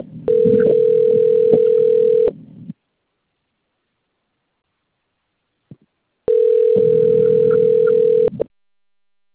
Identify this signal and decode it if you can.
Ringback tone (North America), 440 + 480 Hz.